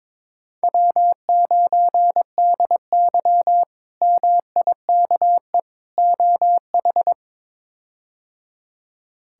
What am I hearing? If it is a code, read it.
Morse code, heard as a keyed tone, about 22 wpm, 700 Hz. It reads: W9DY MIKE OH
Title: K1USN SST QSO Practice - 22wpm